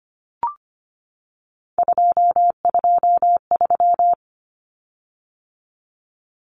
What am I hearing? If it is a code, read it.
Morse code, heard as a keyed tone, about 25 wpm, 700 Hz. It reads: 223
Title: Common Cut Numbers - 3-Digit 25wpm